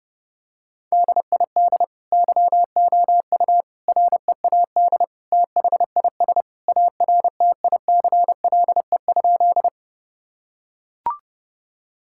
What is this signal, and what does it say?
Morse code, heard as a keyed tone, about 32 wpm, 700 Hz. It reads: DID YOU READ THIS ARTICLE?
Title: Sentences from Top 1000 Words - Review Part 5 - 30wpm